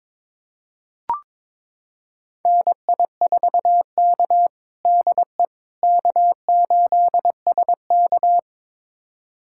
Morse code, heard as a keyed tone, about 22 wpm, 700 Hz. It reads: NI4K DE K8SK
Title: USA Callsign DE Callsign - 4 Characters - 22wpm